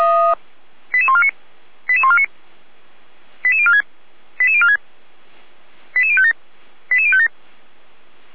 Five-tone selective call (ZVEI), heard as quick runs of five tones, tones of about 70 ms. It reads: 88149 88149 88046 88046 88057 88057